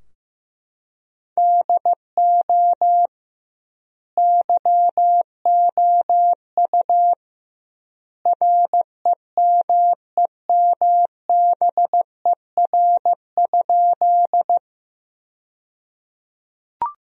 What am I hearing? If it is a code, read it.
Morse code, heard as a keyed tone, about 15 wpm, 700 Hz. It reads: DO YOU REMEMBER?
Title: 2x Word-Spacing - Sentences from Top 400 Words - 15wpm